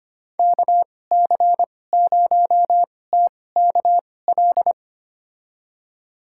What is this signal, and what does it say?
Morse code, heard as a keyed tone, about 25 wpm, spaced as for 18 wpm, 700 Hz. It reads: KC0TKL
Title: USA Call Signs - 6 Characters - 2x Character Spacing - 25wpm